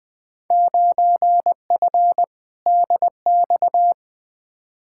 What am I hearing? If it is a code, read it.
Morse code, heard as a keyed tone, about 20 wpm, 700 Hz. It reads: 9F DX
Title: Field Day Exchanges Encore - 20wpm